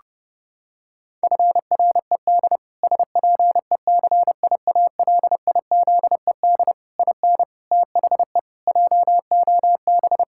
Morse code, heard as a keyed tone, about 30 wpm, 700 Hz. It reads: FRED SPECIALIZED IN THE JOB
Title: Pangram Challenge v2 - 30wpm